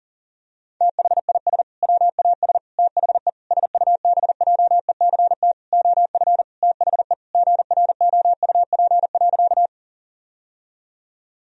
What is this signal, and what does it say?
Morse code, heard as a keyed tone, about 40 wpm, 700 Hz. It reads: THIS WAS THE SUBJECT OF THE GROUP.
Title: Sentences from Top 1000 Words - Review Part 1 - 40wpm